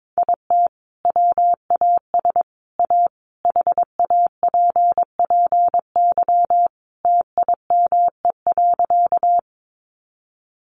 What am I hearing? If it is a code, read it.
Morse code, heard as a keyed tone, about 22 wpm, 700 Hz. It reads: IT WAS A HAPPY TIME.
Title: Sentences from Top 1000 Words 22wpm